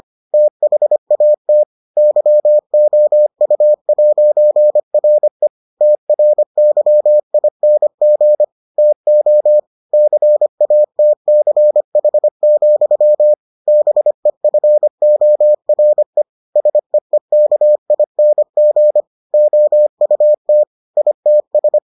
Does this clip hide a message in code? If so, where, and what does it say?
Yes: Morse code from 0.3 to 21.8 s, heard as a keyed tone, about 25 wpm, 600 Hz. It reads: THAT YOU'RE TRYING TO CATCH, BEFORE SEEKING OUT ITS